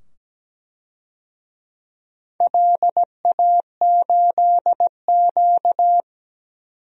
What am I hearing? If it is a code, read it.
Morse code, heard as a keyed tone, about 17 wpm, 700 Hz. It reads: LA8Q
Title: DXCC-100 Call Signs - 4 Characters - 17wpm